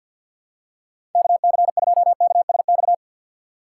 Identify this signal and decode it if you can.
Morse code, heard as a keyed tone, about 51 wpm, 700 Hz. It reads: KC2KSX
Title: USA Call Signs - 6 Characters - 4x Character Spacing - 50wpm